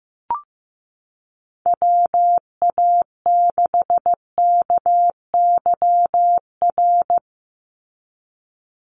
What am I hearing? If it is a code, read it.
Morse code, heard as a keyed tone, about 15 wpm, 700 Hz. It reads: WA6KYR
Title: NA SOTA Chaser Callsigns 15wpm